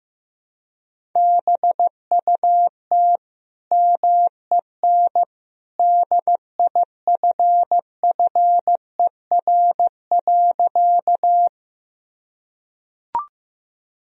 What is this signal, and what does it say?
Morse code, heard as a keyed tone, about 15 wpm, 700 Hz. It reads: BUT MEN DIFFER.
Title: Sentences from Top 1000 Words - Review Part 1 - 15wpm